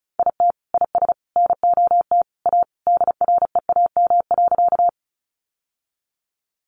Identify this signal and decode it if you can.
Morse code, heard as a keyed tone, about 34 wpm, 700 Hz. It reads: IT IS NOT A DREAM.
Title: Sentences from Top 1500 Words 35wpm